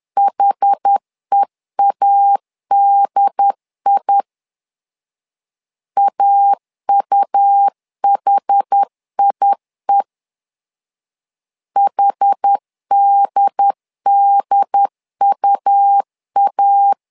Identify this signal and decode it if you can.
Morse code, heard as a keyed tone, about 10 wpm, 800 Hz. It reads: HEADI AUHIE HDDUA
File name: LEZIONI / 2 UHDN / 2 UHD 10wpm 1826